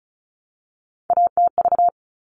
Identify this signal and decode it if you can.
Morse code, heard as a keyed tone, about 34 wpm, 700 Hz. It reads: ATV
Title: All Cut Numbers - 3-Digit 35wpm